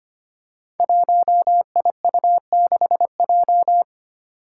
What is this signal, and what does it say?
Morse code, heard as a keyed tone, about 25 wpm, 700 Hz. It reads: 1IU6J
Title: Five Challenging Letter-Number Combinations 25wpm